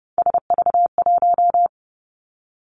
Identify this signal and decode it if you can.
Morse code, heard as a keyed tone, about 30 wpm, 700 Hz. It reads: SV1
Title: Callsign Prefixes for Top 50 DXCC - 30wpm